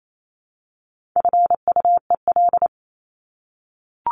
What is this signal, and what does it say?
Morse code, heard as a keyed tone, about 28 wpm, 700 Hz. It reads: FUEL
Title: Top 1751-2000 Words 28wpm